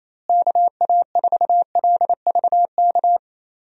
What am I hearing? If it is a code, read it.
Morse code, heard as a keyed tone, about 28 wpm, 700 Hz. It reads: KA4LVK
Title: USA Call Signs - 6 Characters - Repeated 3x - 28wpm